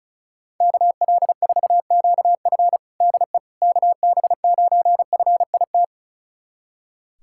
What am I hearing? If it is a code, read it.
Morse code, heard as a keyed tone, about 34 wpm, 700 Hz. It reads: KL4QF DE KB9FIT
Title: USA Callsign DE Callsign 35wpm